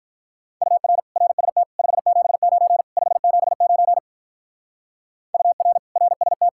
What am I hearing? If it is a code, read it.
Morse code, heard as a keyed tone, about 44 wpm, 700 Hz. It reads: UR RST 579 579 UR RST